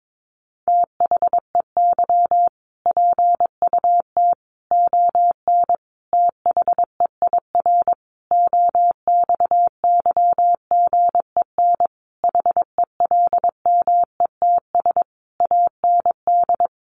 Morse code, heard as a keyed tone, about 22 wpm, 700 Hz. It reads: THEY PUT ON THEIR OXYGEN HELMETS AND
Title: A Sound Of Thunder 22wpm